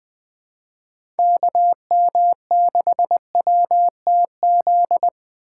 Morse code, heard as a keyed tone, about 21 wpm, 700 Hz. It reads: KM6WTZ